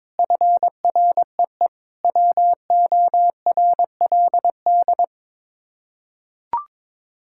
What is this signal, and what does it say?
Morse code, heard as a keyed tone, about 22 wpm, 700 Hz. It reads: FREE WORLD
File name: practice / 22wpm / Sets of 6 Words - 22wpm